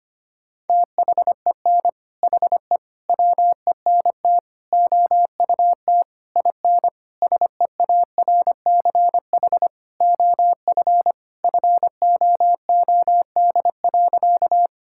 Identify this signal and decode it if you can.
Morse code, heard as a keyed tone, about 25 wpm, 700 Hz. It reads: THEN HE WENT OUT IN SEARCH OF FOOD.
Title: Sentences from Top 1000 Words - Review Part 5 - 25wpm